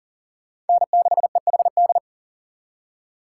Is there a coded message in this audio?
Yes: Morse code from 0.7 to 2.0 s, heard as a keyed tone, about 51 wpm, 700 Hz. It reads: N6EHD